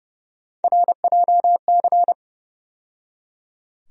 Morse code, heard as a keyed tone, about 30 wpm, 700 Hz. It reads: RJC
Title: USA Call Sign Postfixes - 3 Letters 30wpm